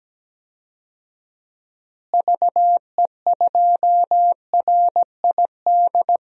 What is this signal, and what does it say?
Morse code, heard as a keyed tone, about 17 wpm, 700 Hz. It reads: VE2RID
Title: DXCC-100 Call Signs - 6 Characters - Repeated 3x - 17wpm